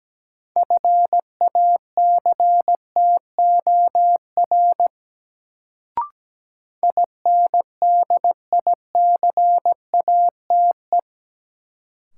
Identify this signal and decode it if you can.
Morse code, heard as a keyed tone, about 17 wpm, 700 Hz. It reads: FACTOR INDICATE
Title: Sentences from Top 1300 Words - New Word Drill - 17wpm